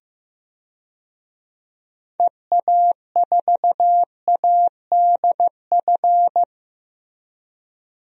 Morse code, heard as a keyed tone, about 15 wpm, 700 Hz. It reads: EA4ADF
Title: DXCC-100 Call Signs - 6 Characters - Repeated 2x - 15wpm